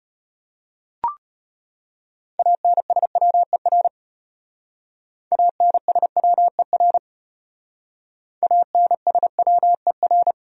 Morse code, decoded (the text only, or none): ANSWER ANSWER ANSWER